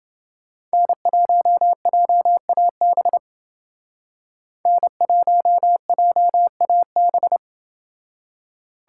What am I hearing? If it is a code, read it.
Morse code, heard as a keyed tone, about 28 wpm, 700 Hz. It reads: N1JAB N1JAB